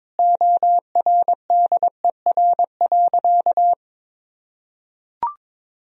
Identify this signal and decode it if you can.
Morse code, heard as a keyed tone, about 22 wpm, 700 Hz. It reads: ORDER.
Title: Sentences from Top 1000 Words - Review Part 2 - 22wpm